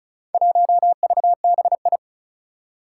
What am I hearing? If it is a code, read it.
Morse code, heard as a keyed tone, about 36 wpm, 690 Hz. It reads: 1VBI
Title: Four Challenging Letter-Number Combinations 35wpm